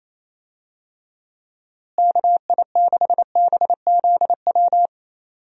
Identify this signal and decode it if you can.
Morse code, heard as a keyed tone, about 28 wpm, 700 Hz. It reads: KI6BZW
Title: USA Call Signs - 6 Characters - Repeated 3x - 28wpm